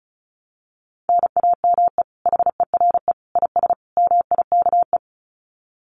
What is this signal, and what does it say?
Morse code, heard as a keyed tone, about 34 wpm, 700 Hz. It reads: NAME HERE IS MIKE
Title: Common QSO Phrases 35wpm